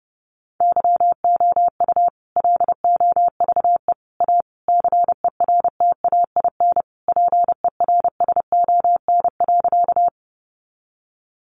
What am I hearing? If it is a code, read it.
Morse code, heard as a keyed tone, about 30 wpm, 700 Hz. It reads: YOU LOVE A CERTAIN PERSON.
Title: Sentences from Top 500 Words 30wpm